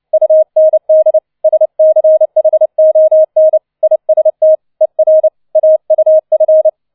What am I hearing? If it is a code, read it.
Morse code, heard as a keyed tone, about 29 wpm, 630 Hz. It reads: UND SCHON IST ER AUF